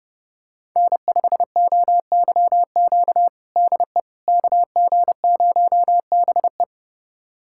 Morse code, heard as a keyed tone, about 30 wpm, 700 Hz. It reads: N5OYQ DE KG0BE
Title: USA Callsign DE Callsign 30wpm